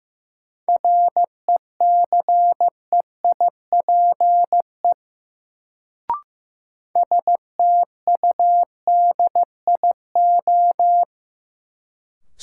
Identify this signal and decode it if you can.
Morse code, heard as a keyed tone, about 16 wpm, 700 Hz. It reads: RECEIPE STUDIO
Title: Sentences from Top 1200 Words - New Word Drill - 15wpm